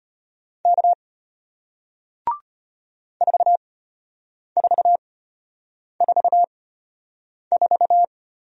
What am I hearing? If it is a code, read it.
Morse code, heard as a keyed tone, about 33 wpm, spaced as for 8 wpm, 700 Hz. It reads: K 4444